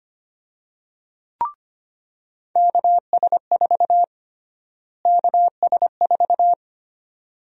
Morse code, heard as a keyed tone, about 25 wpm, 700 Hz. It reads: KS4 KS4
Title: USA Call Sign Prefixes - Repeated 2x - 25wpm